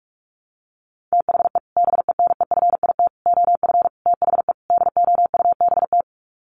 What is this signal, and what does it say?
Morse code, heard as a keyed tone, about 44 wpm, 700 Hz. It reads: THE BENEFIT OF THE DOUBT
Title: Sets of 5 Words Encore 45wpm